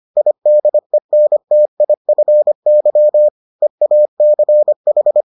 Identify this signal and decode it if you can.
Morse code, heard as a keyed tone, about 25 wpm, 600 Hz. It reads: IDENTIFY EACH